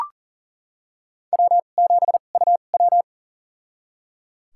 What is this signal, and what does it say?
Morse code, heard as a keyed tone, about 38 wpm, spaced as for 28 wpm, 700 Hz. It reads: W7UW
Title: USA Call Signs - 4 Characters - 2x Character Spacing - 40wpm